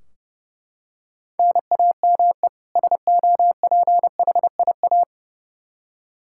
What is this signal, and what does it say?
Morse code, heard as a keyed tone, about 30 wpm, 700 Hz. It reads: NAME SOPHIA
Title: Practice QSO - Exchange 1 - Beginning - 30wpm